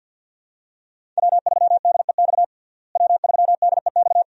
Morse code, heard as a keyed tone, about 50 wpm, 700 Hz. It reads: W3DEX W3DEX
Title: USA Call Signs - 5 Characters - Repeated 2x - 50wpm